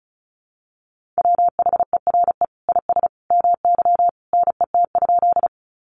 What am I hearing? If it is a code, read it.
Morse code, heard as a keyed tone, about 34 wpm, 700 Hz. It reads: WHERE IS MY NET?
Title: Sentences from Top 1400 Words 35wpm